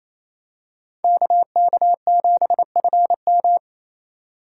Morse code, heard as a keyed tone, about 28 wpm, 700 Hz. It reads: KK7FM
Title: USA Call Signs - 5 Characters - 2x Character Spacing - 28wpm